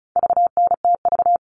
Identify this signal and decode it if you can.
Morse code, heard as a keyed tone, about 34 wpm, 700 Hz. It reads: VNTV
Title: All Cut Numbers - 4-Digit 35wpm